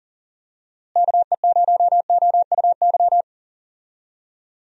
Morse code, heard as a keyed tone, about 40 wpm, 700 Hz. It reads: KE0OUY
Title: USA Call Signs - 6 Characters - Repeated 3x - 40wpm